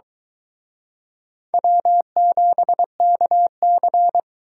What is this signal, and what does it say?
Morse code, heard as a keyed tone, about 23 wpm, 700 Hz. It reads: W7KC